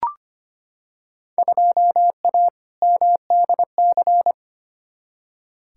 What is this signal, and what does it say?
Morse code, heard as a keyed tone, about 25 wpm, 700 Hz. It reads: 2A MDC
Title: Field Day Exchanges - 25wpm